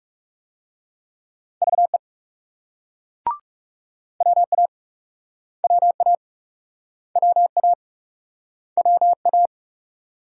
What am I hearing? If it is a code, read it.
Morse code, heard as a keyed tone, about 36 wpm, 700 Hz. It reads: VE WA WA WA WA
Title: Speed-Racing Multi-letter Phonemes - 30wpm